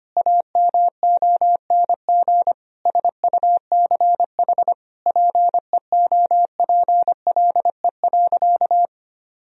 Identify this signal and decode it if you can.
Morse code, heard as a keyed tone, about 25 wpm, 700 Hz. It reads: AMONG SUCH PEOPLE.
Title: Sentences from Top 1000 Words - Review Part 3 - 25wpm